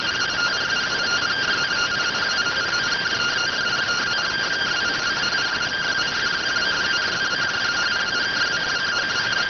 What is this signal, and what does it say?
Two-tone teleprinter signal, about 170 Hz shift.